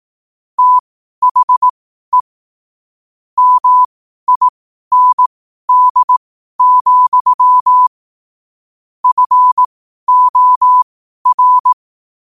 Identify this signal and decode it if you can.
Morse code, heard as a keyed tone, about 18 wpm, spaced as for 12 wpm, 1000 Hz. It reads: THE MIND, FOR
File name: qotdfiles / 12wpm